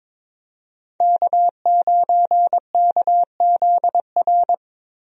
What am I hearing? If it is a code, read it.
Morse code, heard as a keyed tone, about 22 wpm, 700 Hz. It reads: K9KZR
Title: USA Call Signs - 5 Characters - Repeated 2x - 22wpm